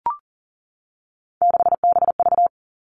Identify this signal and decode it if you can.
Morse code, heard as a keyed tone, about 42 wpm, 700 Hz. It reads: 6BV
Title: Three Challenging Letter-Number Combinations 40wpm